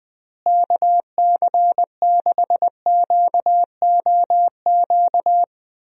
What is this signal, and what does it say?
Morse code, heard as a keyed tone, about 20 wpm, 700 Hz. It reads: KC6QOQ
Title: USA Call Signs - 6 Characters - Repeated 3x - 20wpm